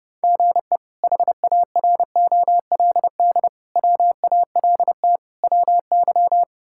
Morse code, heard as a keyed tone, about 30 wpm, 700 Hz. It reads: GE HAROLD WALT WY
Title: K1USN SST QSO Practice - 30wpm